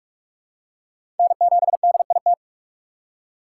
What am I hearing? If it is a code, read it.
Morse code, heard as a keyed tone, about 44 wpm, 700 Hz. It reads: N7DIT
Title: USA Call Signs - 5 Characters - 6x Character Spacing - 45wpm